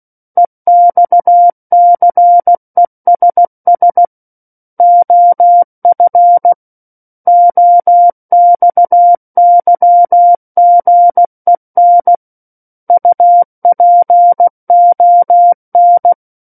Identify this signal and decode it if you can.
Morse code, heard as a keyed tone, about 16 wpm, 700 Hz. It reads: EXCESS OF OXYGEN UPON
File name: Book One / 07-Chapter 7 - 16 WPM